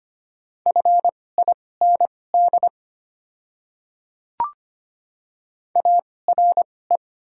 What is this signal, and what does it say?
Morse code, heard as a keyed tone, about 25 wpm, spaced as for 18 wpm, 690 Hz. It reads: FIND ARE
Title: 2x Character Spacing - Top 100 Words - 25wpm